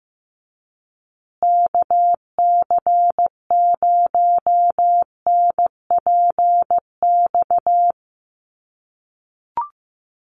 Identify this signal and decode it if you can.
Morse code, heard as a keyed tone, about 15 wpm, 700 Hz. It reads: KC0NPX
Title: USA Call Signs Repeated 2x 15wpm